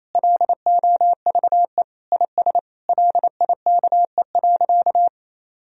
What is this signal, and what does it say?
Morse code, heard as a keyed tone, about 28 wpm, 700 Hz. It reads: LOVE IS LIKE.
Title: Sentences from Top 1000 Words - Review Part 3 - 28wpm